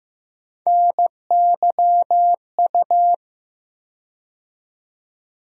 Morse code, heard as a keyed tone, about 15 wpm, 700 Hz. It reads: NYU
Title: USA Call Sign Postfixes - 3 Letters 15wpm